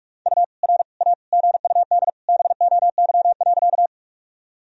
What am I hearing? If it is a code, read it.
Morse code, heard as a keyed tone, about 42 wpm, 700 Hz. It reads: U R A GUD BOY.
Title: Sentences from Top 200 Words and Abbreviations 45wpm